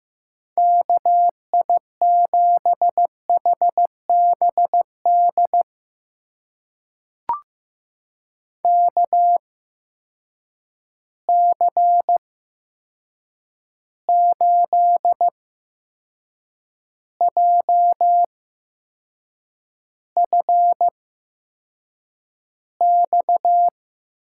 Morse code, heard as a keyed tone, about 15 wpm, spaced as for 6 wpm, 700 Hz. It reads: KI7HBD KC8JFX
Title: USA Call Signs - 6 Characters - 8x Character Spacing - 15wpm